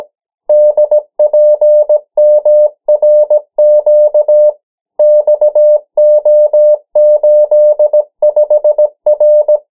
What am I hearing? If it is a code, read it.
Morse code, heard as a keyed tone, about 17 wpm, 600 Hz. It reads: DPMRQ XO85R
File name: morse mp3 / 14 urok 13